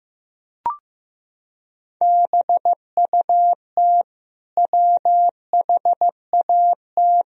Morse code, heard as a keyed tone, about 16 wpm, 700 Hz. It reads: BUT WHAT
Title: Sentences from Top 100 Words and Abbreviations 15wpm